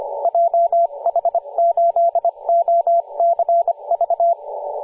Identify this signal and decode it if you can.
Morse code, heard as a keyed tone, about 25 wpm, 680 Hz, with radio noise. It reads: JH8OCV
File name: posts / 2016-aug-17-0039z-3w4xx-14-022mhz-cw